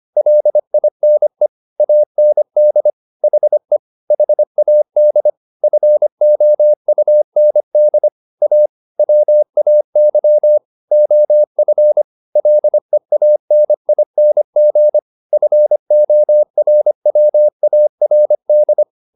Morse code, heard as a keyed tone, about 25 wpm, 600 Hz. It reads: LINE AND HE HAD FOUND A WAY OF LEANING FORWARD
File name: the old man and the sea part 1 25wpm